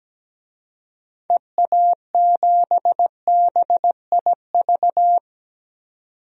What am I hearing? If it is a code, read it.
Morse code, heard as a keyed tone, about 17 wpm, 700 Hz. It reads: EA7BIV
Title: DXCC-100 Call Signs - 6 Characters - 17wpm